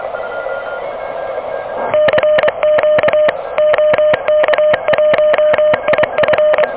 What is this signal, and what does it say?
Morse code, heard as a keyed tone, about 25 wpm, 600 Hz, with noise. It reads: CQ OK1IF
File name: OK QRP 2023 / OK QRP 2023